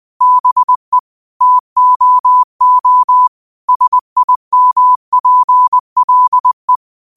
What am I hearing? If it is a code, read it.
Morse code, heard as a keyed tone, about 20 wpm, 1000 Hz. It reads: BE TOO SIMPLE
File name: qotdfiles / 20wpm